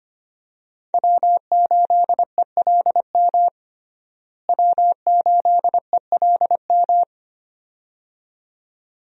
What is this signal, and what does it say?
Morse code, heard as a keyed tone, about 25 wpm, 700 Hz. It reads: W8ELM W8ELM
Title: USA Call Signs - 5 Characters - Repeated 2x - 25wpm